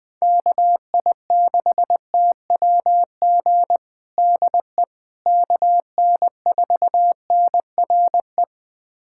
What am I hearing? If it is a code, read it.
Morse code, heard as a keyed tone, about 20 wpm, 700 Hz. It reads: KI6TWG DE KN4NRE
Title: USA Callsign DE Callsign - 6 Characters - 20wpm